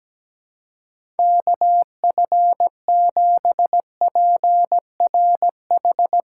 Morse code, heard as a keyed tone, about 17 wpm, 700 Hz. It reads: KF7PRH
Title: USA Call Signs - 6 Characters - 4x Character Spacing - 17wpm